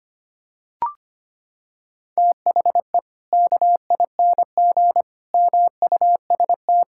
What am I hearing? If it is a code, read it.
Morse code, heard as a keyed tone, about 25 wpm, 700 Hz. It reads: THE KING MUST
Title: Sentences from Top 1000 Words - Review Part 5 - 25wpm